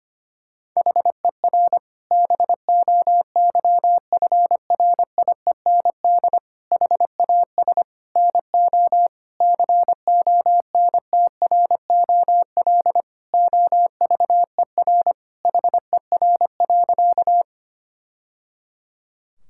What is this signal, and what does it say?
Morse code, heard as a keyed tone, about 25 wpm, 700 Hz. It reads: HER BOYFRIEND HAS NO CONTROL OVER HER.
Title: Sentences from Top 1000 Words - Review Part 5 - 25wpm